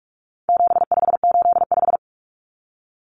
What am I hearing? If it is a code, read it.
Morse code, heard as a keyed tone, about 44 wpm, 700 Hz. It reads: 7585
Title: Common Cut Numbers - 4-Digit 45wpm